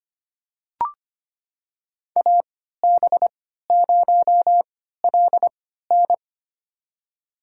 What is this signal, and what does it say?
Morse code, heard as a keyed tone, about 25 wpm, spaced as for 14 wpm, 700 Hz. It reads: AB0LN
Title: USA Call Signs - 5 Characters - 3x Character Spacing - 25wpm